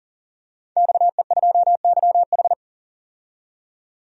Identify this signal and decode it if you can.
Morse code, heard as a keyed tone, about 40 wpm, 700 Hz. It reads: XE2YH
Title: DXCC-100 Call Signs - 5 Characters - 40wpm